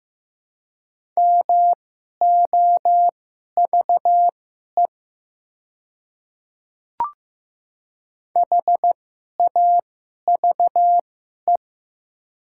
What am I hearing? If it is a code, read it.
Morse code, heard as a keyed tone, about 15 wpm, spaced as for 11 wpm, 700 Hz. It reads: MOVE HAVE
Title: 2x Character Spacing - Top 300 Words - 15wpm